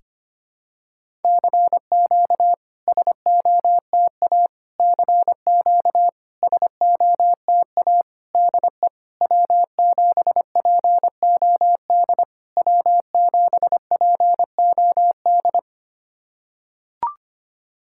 Morse code, heard as a keyed tone, about 25 wpm, 690 Hz. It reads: CQ SOTA CQ SOTA DE W7POD W7POD
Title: SOTA QSO Practice 25wpm